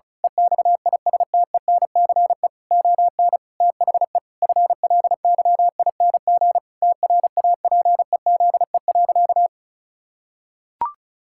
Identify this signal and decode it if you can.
Morse code, heard as a keyed tone, about 36 wpm, 700 Hz. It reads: EXISTENCE ON THE FLYING TRAPEZE.
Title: Pangram Challenge v2 - 35wpm